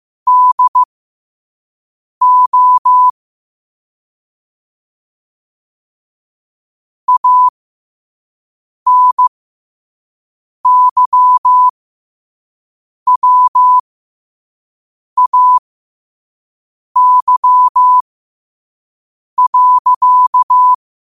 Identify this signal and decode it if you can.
Morse code, heard as a keyed tone, about 15 wpm, spaced as for 5 wpm, 1000 Hz. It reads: DO ANYWAY.
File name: qotdfiles / 5wpm